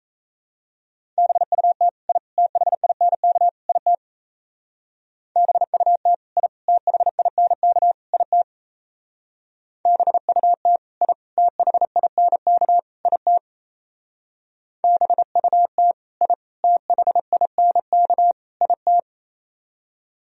Morse code, heard as a keyed tone, about 35 wpm, 700 Hz. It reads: BUT I THINK IT BUT I THINK IT BUT I THINK IT BUT I THINK IT